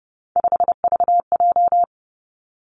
Morse code, heard as a keyed tone, about 31 wpm, 700 Hz. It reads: 5VJ
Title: Three Challenging Letter-Number Combinations 30wpm